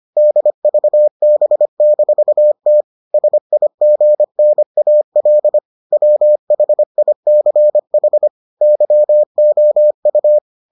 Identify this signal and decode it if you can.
Morse code, heard as a keyed tone, about 25 wpm, 600 Hz. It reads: DVB-T SIGNAL WHICH YOU